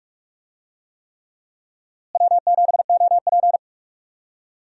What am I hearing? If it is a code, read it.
Morse code, heard as a keyed tone, about 47 wpm, 700 Hz. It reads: W7OP
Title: USA Call Signs - 4 Characters - Repeated 3x - 45wpm